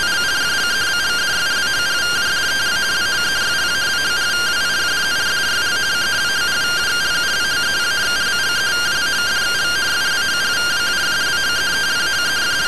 Two-tone teleprinter signal, about 170 Hz shift.